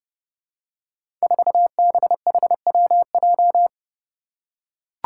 Morse code, heard as a keyed tone, about 30 wpm, 700 Hz. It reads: 4BHWJ
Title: Five Challenging Letter-Number Combinations 30wpm